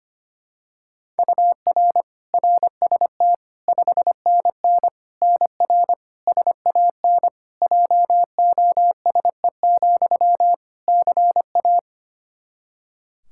Morse code, heard as a keyed tone, about 25 wpm, 700 Hz. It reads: UR RST 5NN NR SAN JOSE, CA
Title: Practice QSO - Exchange 1 - Beginning - 25wpm